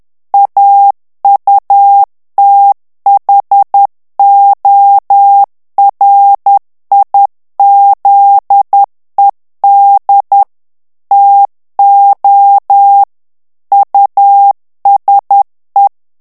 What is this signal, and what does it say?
Morse code, heard as a keyed tone, about 10 wpm, 800 Hz. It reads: AUTHORIZED TO USE
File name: ANGLAIS texte clair 3 vitesse 10 mots minute